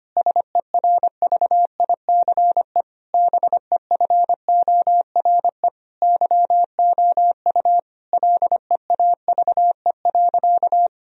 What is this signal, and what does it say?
Morse code, heard as a keyed tone, about 25 wpm, 700 Hz. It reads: SERVICE BEFORE YOU LEAVE.
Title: Sentences from Top 1000 Words - Review Part 4 - 25wpm